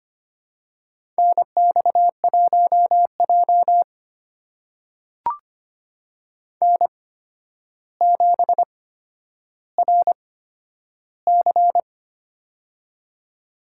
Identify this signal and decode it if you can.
Morse code, heard as a keyed tone, about 25 wpm, spaced as for 11 wpm, 700 Hz. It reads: NX1J N7RC
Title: USA Call Signs - 4 Characters - 8x Character Spacing - 25wpm